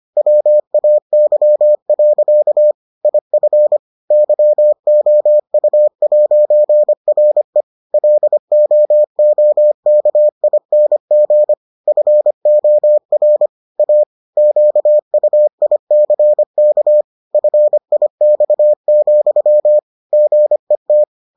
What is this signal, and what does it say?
Morse code, heard as a keyed tone, about 25 wpm, 600 Hz. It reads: WAY. IF YOU'RE LOOKING FOR A QUICK FIX, GET